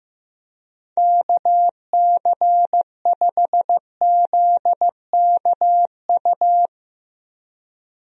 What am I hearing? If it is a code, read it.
Morse code, heard as a keyed tone, about 15 wpm, 700 Hz. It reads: KC5ZKU
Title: USA Call Signs - 6 Characters - Repeated 3x - 15wpm